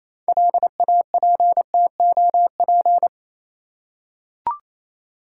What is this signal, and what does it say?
Morse code, heard as a keyed tone, about 27 wpm, 690 Hz. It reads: LAPTOP
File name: Easy Words - Speed-Racing with Hints - 22wpm